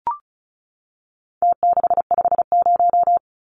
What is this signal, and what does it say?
Morse code, heard as a keyed tone, about 34 wpm, 700 Hz. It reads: T650
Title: Numbers - Contest Style - 4-Digit 35wpm